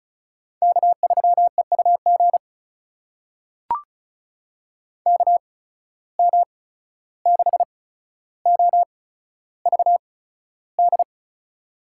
Morse code, heard as a keyed tone, about 36 wpm, spaced as for 14 wpm, 700 Hz. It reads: K3EUG KM6OVD
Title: USA Call Signs - 8x Character Spacing - 35wpm